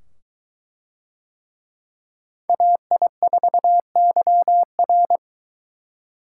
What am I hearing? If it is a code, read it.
Morse code, heard as a keyed tone, about 23 wpm, 700 Hz. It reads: AI4YR